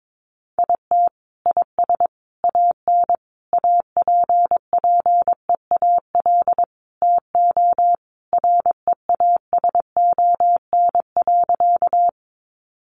Morse code, heard as a keyed tone, about 22 wpm, 700 Hz. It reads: IT IS AN APPEAL TO REASON.
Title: Sentences from Top 1300 Words 22wpm